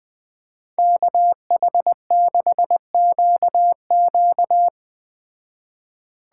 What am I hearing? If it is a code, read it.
Morse code, heard as a keyed tone, about 20 wpm, 700 Hz. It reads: KH6QQ
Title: Speed Racing - USA 2x2 Call Signs 17wpm